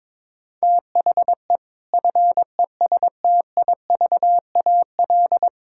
Morse code, heard as a keyed tone, about 22 wpm, 700 Hz. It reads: THE FESTIVAL
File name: Difficult Sentences with Hints - 22wpm